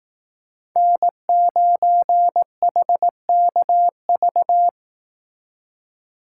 Morse code, heard as a keyed tone, about 18 wpm, 700 Hz. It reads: N9HKV
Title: Speed Racing - USA 1x3 Call Signs 15wpm